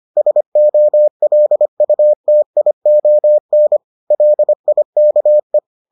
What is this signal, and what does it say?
Morse code, heard as a keyed tone, about 25 wpm, 600 Hz. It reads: SOLUTION LIKE